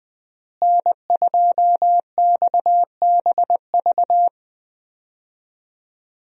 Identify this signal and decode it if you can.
Morse code, heard as a keyed tone, about 20 wpm, 700 Hz. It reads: N2XBV
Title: USA Call Signs - 5 Characters - Repeated 3x - 20wpm